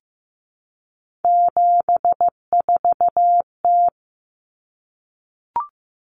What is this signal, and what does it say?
Morse code, heard as a keyed tone, about 15 wpm, 700 Hz. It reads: 74T
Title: Common Cut Numbers - 3-Digit 15wpm